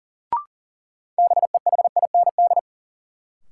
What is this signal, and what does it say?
Morse code, heard as a keyed tone, about 40 wpm, 700 Hz. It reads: BEHIND
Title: Sentences from Top 500 Words - New Word Drill - 40wpm